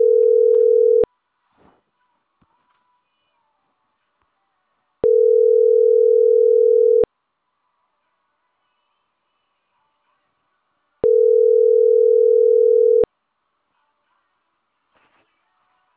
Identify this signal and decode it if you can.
Ringback tone (North America), 440 + 480 Hz.